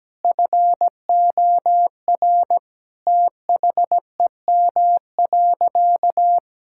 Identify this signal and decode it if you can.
Morse code, heard as a keyed tone, about 17 wpm, 700 Hz. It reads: FOR THEM.
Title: Sentences from Top 1600 Words 17wpm